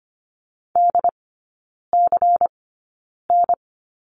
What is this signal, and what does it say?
Morse code, heard as a keyed tone, about 25 wpm, spaced as for 9 wpm, 700 Hz. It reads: DCN
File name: course / 25:10wpm / 093 Lesson 16e - Letter c - Three Characters 25wpm